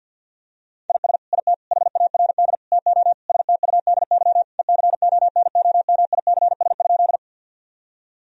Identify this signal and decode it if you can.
Morse code, heard as a keyed tone, about 48 wpm, 700 Hz. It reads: IS IT HARD TO STUDY ECONOMICS?
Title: Sentences from Top 1000 Words - Review Part 3 - 50wpm